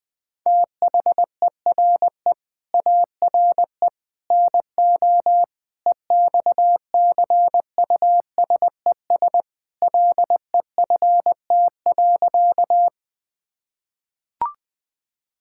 Morse code, heard as a keyed tone, about 20 wpm, 700 Hz. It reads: THERE ARE NO EXCUSES LEFT.
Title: Sentences with 3-Word Phrases - 20wpm